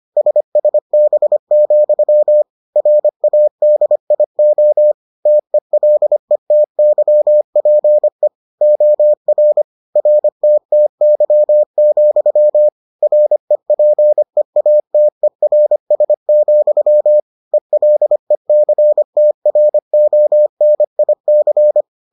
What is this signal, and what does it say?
Morse code, heard as a keyed tone, about 25 wpm, 600 Hz. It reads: SSB, RADIO TELETYPE OR RTTY, REPEATERS, ELECTRONIC